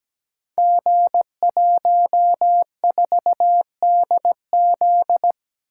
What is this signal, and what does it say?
Morse code, heard as a keyed tone, about 17 wpm, 700 Hz. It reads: G14DZ
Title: Code Groups - 5 Letter:Numbers - 17wpm